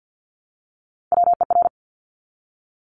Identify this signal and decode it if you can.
Morse code, heard as a keyed tone, about 50 wpm, 700 Hz. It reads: WER
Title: Top Three Letter Combinations in Words 50wpm